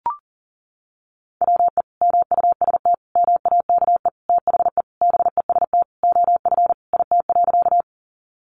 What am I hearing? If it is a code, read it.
Morse code, heard as a keyed tone, about 40 wpm, 700 Hz. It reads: WE MUST MAKE THE BEST OF IT.
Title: Sentences from Top 400 Words 40wpm